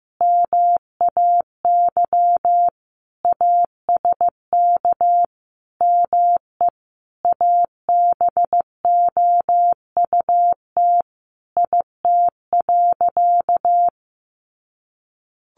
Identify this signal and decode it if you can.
Morse code, heard as a keyed tone, about 15 wpm, 700 Hz. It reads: MAY ASK ME ABOUT IT.
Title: Sentences from Top 200 Words 15wpm